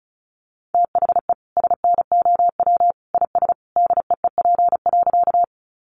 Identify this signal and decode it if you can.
Morse code, heard as a keyed tone, about 34 wpm, 700 Hz. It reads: THE SNOW IS DEEP.
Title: Sentences from Top 500 Words 35wpm